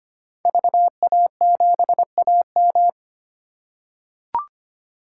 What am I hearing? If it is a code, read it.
Morse code, heard as a keyed tone, about 25 wpm, 700 Hz. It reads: VA7AM
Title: K1USN SST QSO Practice - 25wpm